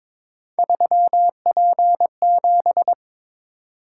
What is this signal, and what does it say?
Morse code, heard as a keyed tone, about 22 wpm, 700 Hz. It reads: 3P7
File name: practice / 22wpm / Three Letter-Number Combinations - 22wpm